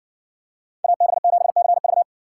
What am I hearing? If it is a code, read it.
Morse code, heard as a keyed tone, about 60 wpm, 700 Hz. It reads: AB7CV